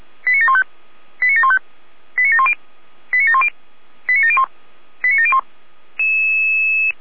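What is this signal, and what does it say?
Five-tone selective call (ZVEI), heard as quick runs of five tones, tones of about 70 ms. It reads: 89715 89715 89710 89710 89701 89701
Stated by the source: Received by SDR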